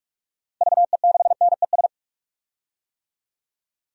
Morse code, heard as a keyed tone, about 46 wpm, 700 Hz. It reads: VE6NES
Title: DXCC-100 Call Signs - 6 Characters - Repeated 3x - 45wpm